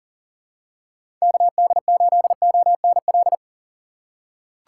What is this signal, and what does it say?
Morse code, heard as a keyed tone, about 40 wpm, 700 Hz. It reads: KD8ONL
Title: USA Call Signs - 6 Characters - 6x Character Spacing - 40wpm